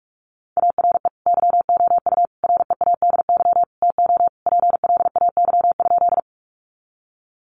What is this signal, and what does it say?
Morse code, heard as a keyed tone, about 44 wpm, 700 Hz. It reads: ARE YOU READY TO PLAY?
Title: Sentences from Top 400 Words 45wpm